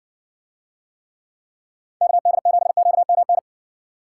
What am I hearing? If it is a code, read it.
Morse code, heard as a keyed tone, about 61 wpm, 700 Hz. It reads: KD7QMN